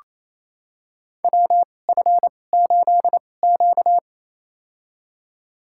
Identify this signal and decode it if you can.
Morse code, heard as a keyed tone, about 28 wpm, spaced as for 20 wpm, 700 Hz. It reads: WF8Q
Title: USA Call Signs - 4 Characters - 2x Character Spacing - 28wpm